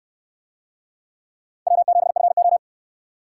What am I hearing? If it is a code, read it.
Morse code, heard as a keyed tone, about 67 wpm, 700 Hz. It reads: W7FO